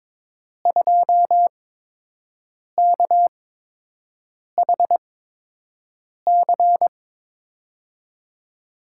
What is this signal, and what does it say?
Morse code, heard as a keyed tone, about 22 wpm, spaced as for 6 wpm, 700 Hz. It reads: 2KHC
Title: USA Call Signs - 5 Characters - 8x Character Spacing - 22wpm